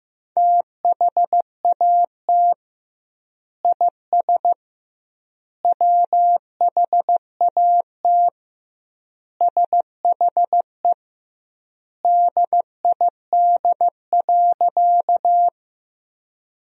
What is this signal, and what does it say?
Morse code, heard as a keyed tone, about 16 wpm, 700 Hz. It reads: THAT IS WHAT SHE DID.
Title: 2x Word-Spacing - Sentences from Top 100 Words - 15wpm